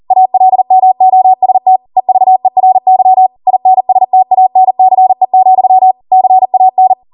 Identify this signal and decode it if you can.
Morse code, heard as a keyed tone, about 40 wpm, 750 Hz. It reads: ALMOST EVERY INSTANCE, CAN